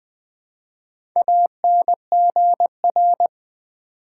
Morse code, heard as a keyed tone, about 20 wpm, 700 Hz. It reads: ANGR
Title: Build A Word - 1 Letter At A Time - 3 to 5-Letter Words - 20wpm